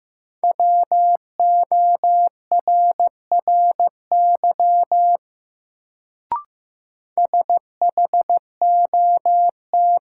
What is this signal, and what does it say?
Morse code, heard as a keyed tone, about 15 wpm, 700 Hz. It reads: WORRY SHOT
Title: Sentences from Top 1100 Words - New Word Drill - 15wpm